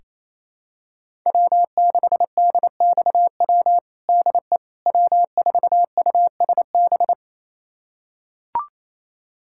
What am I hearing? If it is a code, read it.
Morse code, heard as a keyed tone, about 28 wpm, 700 Hz. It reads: W6DXW DE W4USB
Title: USA Callsign DE Callsign - 5 Characters - 28wpm